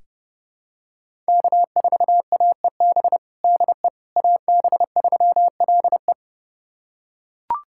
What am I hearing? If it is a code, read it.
Morse code, heard as a keyed tone, about 30 wpm, 700 Hz. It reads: K4AEB DE AB3LE
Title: USA Callsign DE Callsign - 5 Characters - 30wpm